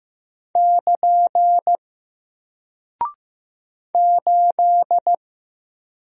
Morse code, heard as a keyed tone, about 15 wpm, 700 Hz. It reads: <KN> 8
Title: Single Character (Letter:Number:Punctuation:Prosign) Alt 15wpm